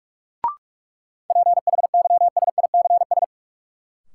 Morse code, heard as a keyed tone, about 46 wpm, 700 Hz. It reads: PHYSICS
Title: Sentences from Top 1000 Words - Review Part 5 - New Word Drill - 45wpm